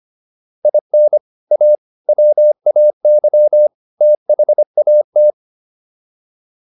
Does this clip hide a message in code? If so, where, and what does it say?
Yes: Morse code from 0.7 to 5.3 s, heard as a keyed tone, about 25 wpm, 600 Hz. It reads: IN A WAY THAT